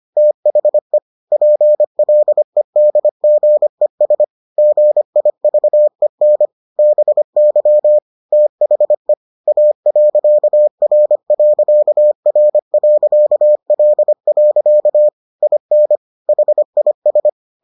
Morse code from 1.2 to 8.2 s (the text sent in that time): PLEDGES GIVEN BY